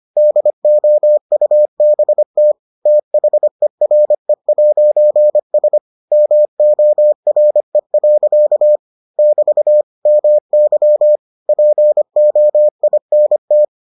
Morse code, heard as a keyed tone, about 25 wpm, 600 Hz. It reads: DOUBT THERE'S MORE. <BT> MY POINT